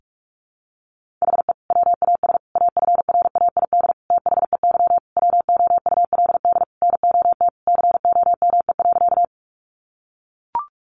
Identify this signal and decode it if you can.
Morse code, heard as a keyed tone, about 44 wpm, 700 Hz. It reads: HE WAS AFRAID THEY WOULD NOT COME.
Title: Sentences from Top 1500 Words 45wpm